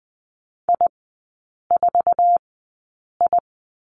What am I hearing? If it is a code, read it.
Morse code, heard as a keyed tone, about 20 wpm, spaced as for 8 wpm, 700 Hz. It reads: I4I
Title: 093 Lesson 16e - Letter c - Three Characters 20wpm